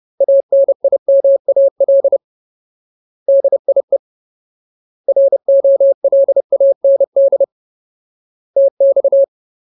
Morse code, heard as a keyed tone, about 30 wpm, 550 Hz. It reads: ANIMAL DIE ROLAND TX
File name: Ultimate Word List / 30WPM Ultimate